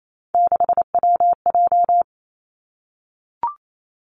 Morse code, heard as a keyed tone, about 28 wpm, 700 Hz. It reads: 6WJ
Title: Three Challenging Letter-Number Combinations 28wpm